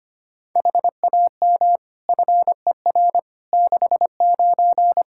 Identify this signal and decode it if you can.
Morse code, heard as a keyed tone, about 25 wpm, 700 Hz. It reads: HAM FER 69
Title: Practice QSO - Exchange 3 - More Info - 25wpm